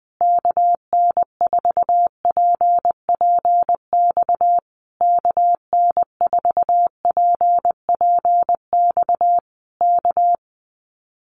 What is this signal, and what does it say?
Morse code, heard as a keyed tone, about 20 wpm, 700 Hz. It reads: KN4PPX KN4PPX K
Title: Common QSO Phrases Encore 20wpm